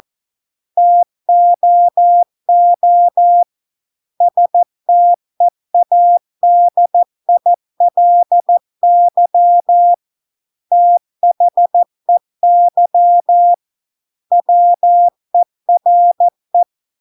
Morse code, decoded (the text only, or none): TOO STEADILY THEY WERE